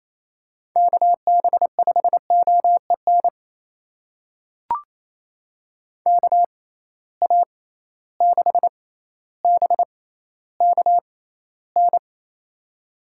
Morse code, heard as a keyed tone, about 28 wpm, spaced as for 14 wpm, 700 Hz. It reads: KB5OEN KA6BKN
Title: USA Call Signs - 6 Characters - 6x Character Spacing - 28wpm